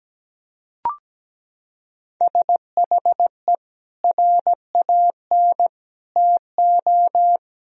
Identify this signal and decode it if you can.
Morse code, heard as a keyed tone, about 17 wpm, 700 Hz. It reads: SHE RAN TO
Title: Sentences from Top 1000 Words - Review Part 3 - 17wpm